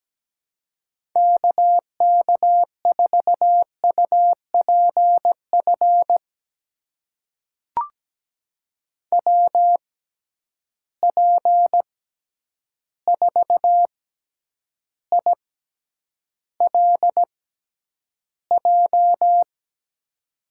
Morse code, heard as a keyed tone, about 17 wpm, spaced as for 9 wpm, 700 Hz. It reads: KK4UPF WP4ILJ
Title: USA Call Signs - 6x Character Spacing - 17wpm